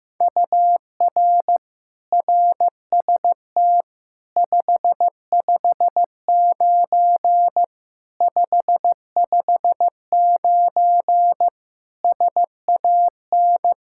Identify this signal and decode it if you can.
Morse code, heard as a keyed tone, about 15 wpm, 700 Hz. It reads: UR RST 559 559 SAN
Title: Practice QSO - Exchange 1 - Beginning - 15wpm